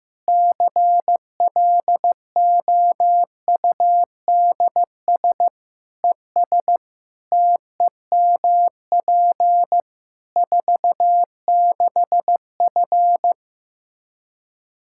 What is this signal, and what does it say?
Morse code, heard as a keyed tone, about 15 wpm, 700 Hz. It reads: CLOUDS ES TEMP 46F
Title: Practice QSO - Exchange 2 - Info - 15wpm